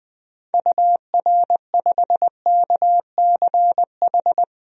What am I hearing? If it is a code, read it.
Morse code, heard as a keyed tone, about 20 wpm, 700 Hz. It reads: UR5KCH
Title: DXCC-100 Call Signs - 6 Characters - 20wpm